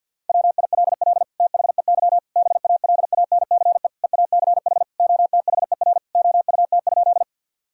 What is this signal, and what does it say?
Morse code, heard as a keyed tone, about 52 wpm, 700 Hz. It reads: WILL THEY BALANCE EACH OTHER OUT?
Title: Sentences from Top 1000 Words - Review Part 6 - 50wpm